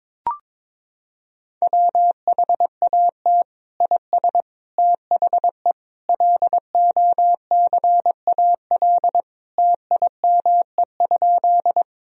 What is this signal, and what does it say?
Morse code, heard as a keyed tone, about 21 wpm, 700 Hz. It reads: WHAT IS THE LOCAL TIME?
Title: Sentences from Top 1000 Words - Review Part 3 - 22wpm